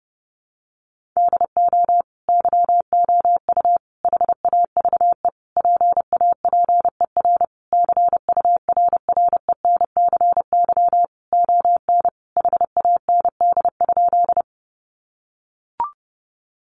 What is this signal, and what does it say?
Morse code, heard as a keyed tone, about 30 wpm, 700 Hz. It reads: DO YOU HAVE PAPER CURRENCY ON HAND?
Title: Sentences from Top 1500 Words 30wpm